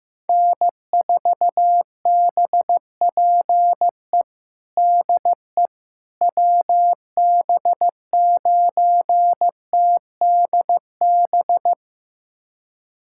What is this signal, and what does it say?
Morse code, heard as a keyed tone, about 16 wpm, 700 Hz. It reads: N4BPE DE WB9TDB
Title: USA Callsign DE Callsign 15wpm